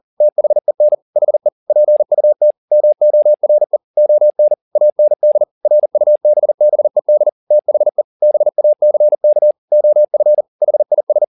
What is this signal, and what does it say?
Morse code, heard as a keyed tone, about 40 wpm, 600 Hz. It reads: THEN HE PUT MORE ON AND RUBBED THE BACK OF HIS
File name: the old man and the sea 2 40wpm